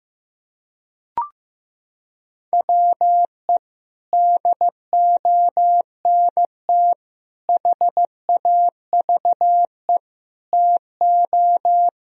Morse code, heard as a keyed tone, about 15 wpm, 700 Hz. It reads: WE DONT HAVE TO
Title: Sentences from Top 1000 Words - Review Part 2 - 15wpm